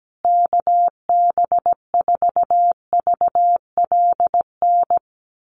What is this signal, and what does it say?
Morse code, heard as a keyed tone, about 17 wpm, 700 Hz. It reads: KB4VLN
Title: USA Call Signs Repeated 2x 17wpm